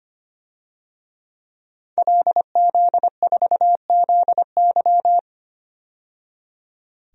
Morse code, heard as a keyed tone, about 25 wpm, 700 Hz. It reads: LZ4ZY
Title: DXCC-100 Call Signs - 5 Characters - 25wpm